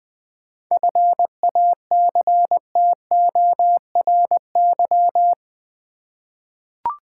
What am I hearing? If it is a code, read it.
Morse code, heard as a keyed tone, about 20 wpm, 700 Hz. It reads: FACTORY
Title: 7 Letter Words - First 3 Letters Given - 20wpm